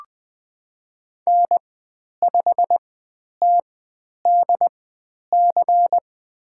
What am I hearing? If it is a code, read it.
Morse code, heard as a keyed tone, about 20 wpm, spaced as for 10 wpm, 700 Hz. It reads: N5TDC
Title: 109 Lesson 18g - Letter m - Call Signs 20wpm